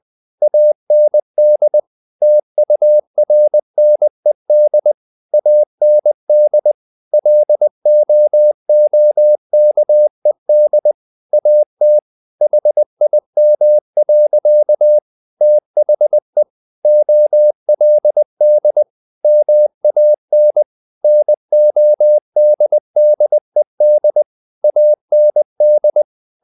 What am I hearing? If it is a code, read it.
Morse code, heard as a keyed tone, about 20 wpm, 600 Hz. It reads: AND TURNED AND LOOKED AT HIM. THE OLD MAN NODDED AND
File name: the old man and the sea / the old man and the sea part 1 20wpm